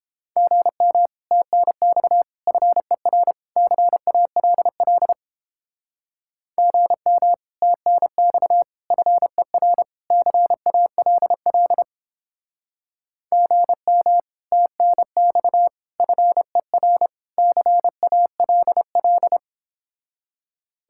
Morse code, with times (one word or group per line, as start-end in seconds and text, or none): 0.4-1.1 GM
1.3-2.2 TNX
2.5-3.3 FER
3.6-5.1 CALL
6.6-7.3 GM
7.6-8.6 TNX
8.9-9.8 FER
10.1-11.8 CALL
13.3-14.2 GM
14.5-15.7 TNX
16.0-17.1 FER
17.4-19.4 CALL